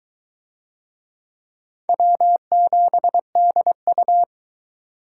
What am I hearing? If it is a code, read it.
Morse code, heard as a keyed tone, about 22 wpm, 700 Hz. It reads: W7DU